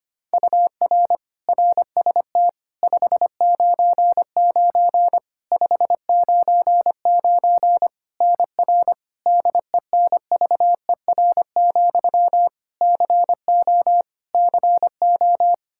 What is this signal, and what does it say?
Morse code, heard as a keyed tone, about 25 wpm, 700 Hz. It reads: UR RST 599 599 NR DENVER, CO CO
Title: Practice QSO - Exchange 1 - Beginning - 25wpm